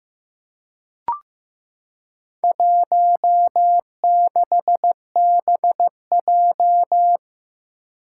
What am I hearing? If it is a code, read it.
Morse code, heard as a keyed tone, about 15 wpm, 700 Hz. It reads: 16BJ
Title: Four Challenging Letter-Number Combinations 15wpm